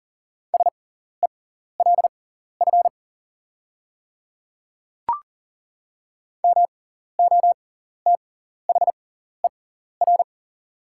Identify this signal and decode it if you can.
Morse code, heard as a keyed tone, about 40 wpm, spaced as for 14 wpm, 690 Hz. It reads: SELF MOTHER
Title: 6x Character Spacing - Top 300 Words - 40wpm